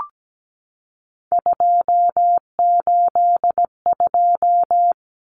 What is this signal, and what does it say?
Morse code, heard as a keyed tone, about 17 wpm, 700 Hz. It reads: 282
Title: Numbers - 3-Digit 17wpm